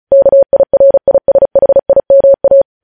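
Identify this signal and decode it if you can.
Morse code, heard as a keyed tone, about 36 wpm, 550 Hz. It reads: KIRISHIMA
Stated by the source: Morse code news headlines at 35.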